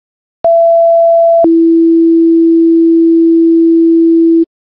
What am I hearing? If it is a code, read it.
Two-tone page, heard as a one-second tone, then a three-second tone, about 670 then 330 Hz, with noise. It is heard from 0.4 to 4.5 s.